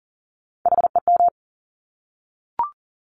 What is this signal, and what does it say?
Morse code, heard as a keyed tone, about 42 wpm, 700 Hz. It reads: HEM
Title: Top Three Letter Combinations in Words 40wpm